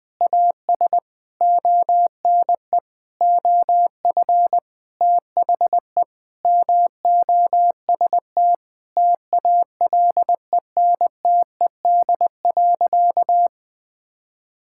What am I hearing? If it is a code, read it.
Morse code, heard as a keyed tone, about 20 wpm, 700 Hz. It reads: AS ONE OF THE MOST TALENTED.
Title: Sentences with 5-Word Phrases - 20wpm